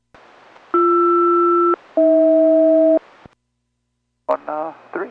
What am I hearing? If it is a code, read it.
Aircraft SELCAL, heard as two pairs of tones, with speech. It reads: BR-AH